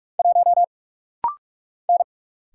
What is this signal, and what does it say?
Morse code, heard as a keyed tone, about 43 wpm, 700 Hz. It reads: 1 N
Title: Single Letter-Number - Rapid-Fire 45wpm